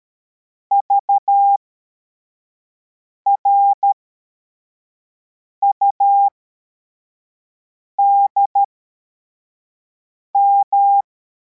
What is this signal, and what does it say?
Morse code, heard as a keyed tone, about 13 wpm, spaced as for 4 wpm, 800 Hz. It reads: VRUDM
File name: formation / cw / Text55